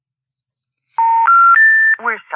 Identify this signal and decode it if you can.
Special information tone, with speech.